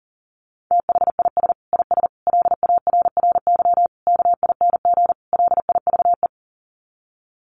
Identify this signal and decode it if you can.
Morse code, heard as a keyed tone, about 37 wpm, 700 Hz. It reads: THIS IS LARRY KING LIVE
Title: Sets of 5 Words Encore2 40wpm